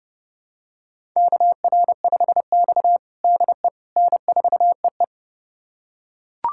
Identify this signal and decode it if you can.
Morse code, heard as a keyed tone, about 30 wpm, 700 Hz. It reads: KR5X DE N4EE
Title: USA Callsign DE Callsign - 4 Characters - 30wpm